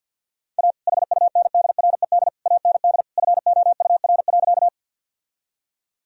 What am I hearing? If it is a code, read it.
Morse code, heard as a keyed tone, about 46 wpm, 700 Hz. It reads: A HUNDRED AND FOUR.
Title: Sentences from Top 1000 Words - Review Part 2 - 50wpm